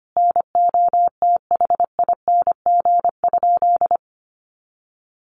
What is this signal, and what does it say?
Morse code, heard as a keyed tone, about 25 wpm, 700 Hz. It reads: NOTHING?
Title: Sentences from Top 500 Words 25wpm